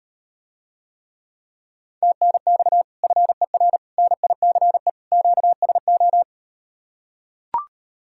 Morse code, heard as a keyed tone, about 38 wpm, 700 Hz. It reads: TNX FER NICE QSO